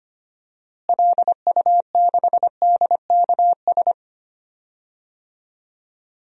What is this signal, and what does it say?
Morse code, heard as a keyed tone, about 25 wpm, 700 Hz. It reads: LU6DKS
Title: DXCC-100 Call Signs - 6 Characters - Repeated 3x - 25wpm